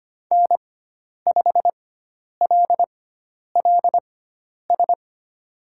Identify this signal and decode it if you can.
Morse code, heard as a keyed tone, about 25 wpm, spaced as for 10 wpm, 700 Hz. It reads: N5LLS
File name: Lesson 16g - Letter c - Call Signs 25wpm